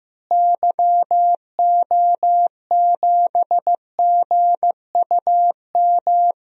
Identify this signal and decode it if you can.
Morse code, heard as a keyed tone, about 15 wpm, 700 Hz. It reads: YO7GUM
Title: DXCC-100 Call Signs - 6 Characters - Repeated 2x - 15wpm